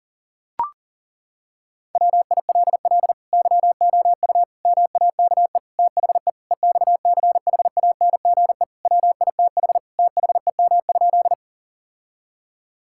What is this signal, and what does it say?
Morse code, heard as a keyed tone, about 40 wpm, 700 Hz. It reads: WILL YOU MAKE THE EXCHANGE WITH THEM?
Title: Sentences from Top 1000 Words - Review Part 6 - 40wpm